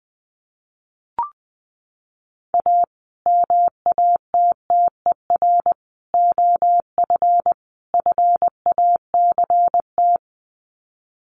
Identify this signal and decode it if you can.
Morse code, heard as a keyed tone, about 19 wpm, 700 Hz. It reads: A MATTER OF FACT
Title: Sets of 4 Words 20wpm